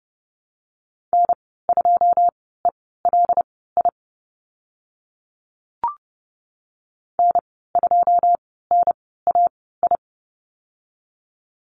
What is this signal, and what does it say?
Morse code, heard as a keyed tone, about 30 wpm, spaced as for 17 wpm, 700 Hz. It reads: N2ELI N2NAI
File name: Lesson 15g - Number 5 - Call Signs 30wpm